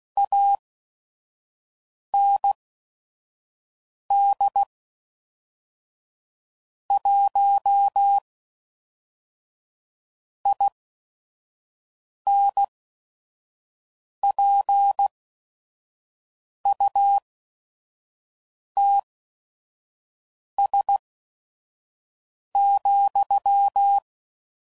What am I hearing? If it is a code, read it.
Morse code, heard as a keyed tone, about 16 wpm, spaced as for 5 wpm, 800 Hz. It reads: AND 1 INPUTS,
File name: CW Code Groups @5WPM / Technical Subjects / Gates Intro